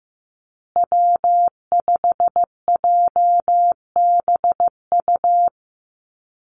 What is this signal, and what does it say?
Morse code, heard as a keyed tone, about 15 wpm, 700 Hz. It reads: W5JBU
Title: USA Call Signs Repeated 3x 15wpm